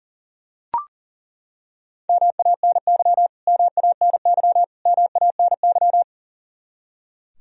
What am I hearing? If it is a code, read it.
Morse code, heard as a keyed tone, about 37 wpm, 700 Hz. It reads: MANY MANY MANY
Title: Top 100 Words Repeated 40wpm